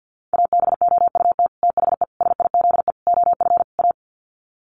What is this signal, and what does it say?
Morse code, heard as a keyed tone, about 50 wpm, 700 Hz. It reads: ABOUT THE SIZE OF A
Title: Sets of 5 Words 50wpm